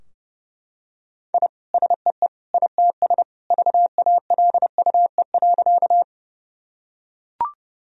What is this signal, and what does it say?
Morse code, heard as a keyed tone, about 32 wpm, 700 Hz. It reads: I SEE ITS VALUE.
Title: Sentences from Top 1000 Words - Review Part 3 - 30wpm